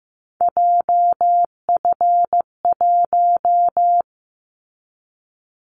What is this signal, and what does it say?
Morse code, heard as a keyed tone, about 15 wpm, 700 Hz. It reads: JF1
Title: Callsign Prefixes for Top 25 DXCC - 15wpm